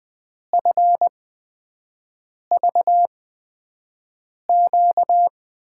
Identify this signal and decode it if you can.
Morse code, heard as a keyed tone, about 20 wpm, spaced as for 5 wpm, 700 Hz. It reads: FVQ
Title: USA Call Signs - 6 Characters - 8x Character Spacing - 20wpm